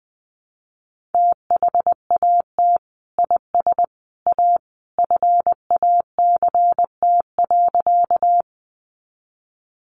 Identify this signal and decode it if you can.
Morse code, heard as a keyed tone, about 20 wpm, 700 Hz. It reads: THAT IS A FACT.
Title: Sentences from Top 500 Words 20wpm